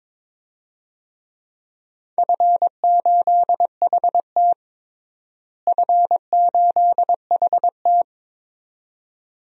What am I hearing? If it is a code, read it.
Morse code, heard as a keyed tone, about 22 wpm, 700 Hz. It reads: F8HT F8HT
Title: DXCC-100 Call Signs - 4 Characters - Repeated 2x - 22wpm